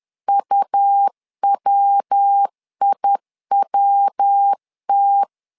Morse code, heard as a keyed tone, about 10 wpm, 800 Hz. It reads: UWIWT
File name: LEZIONI / 3 MWRO / 3 MW 10wpm 1959